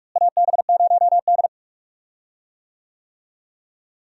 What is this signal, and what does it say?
Morse code, heard as a keyed tone, about 41 wpm, 700 Hz. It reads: AB0D
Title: USA Call Signs - 4 Characters - Repeated 2x - 45wpm